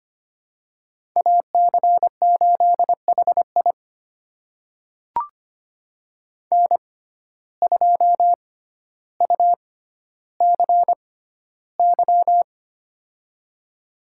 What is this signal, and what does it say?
Morse code, heard as a keyed tone, about 25 wpm, spaced as for 13 wpm, 700 Hz. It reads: AC8HI N2UCY
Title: USA Call Signs - 5 Characters - 6x Character Spacing - 25wpm